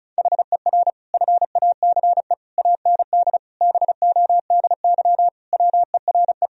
Morse code, heard as a keyed tone, about 36 wpm, 700 Hz. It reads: HER FACE AND BODY WERE
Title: Sentences from Top 1000 Words - Review Part 2 - 35wpm